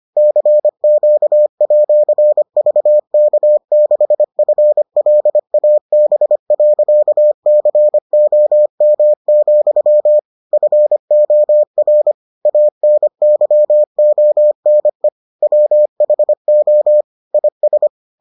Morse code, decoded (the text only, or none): CQ@VK6FLAB.COM, FOR ANYONE WHO IS